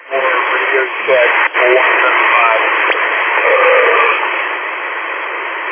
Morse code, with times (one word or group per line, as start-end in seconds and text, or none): none